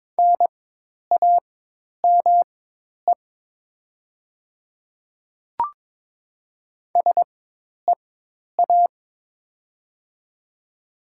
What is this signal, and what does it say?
Morse code, heard as a keyed tone, about 22 wpm, spaced as for 10 wpm, 690 Hz. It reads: NAME SEA
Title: 4x Character Spacing - Top 300 Words - 22wpm